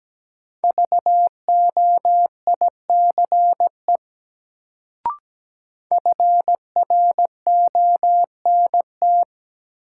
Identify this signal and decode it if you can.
Morse code, heard as a keyed tone, about 17 wpm, 700 Hz. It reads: VOICE FRONT
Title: Sentences from Top 500 Words - New Word Drill - 17wpm